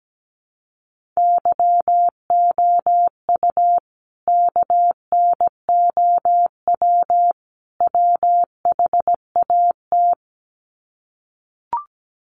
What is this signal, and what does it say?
Morse code, heard as a keyed tone, about 17 wpm, 700 Hz. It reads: YOU KNOW WHAT
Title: Sets of 3 Words 17wpm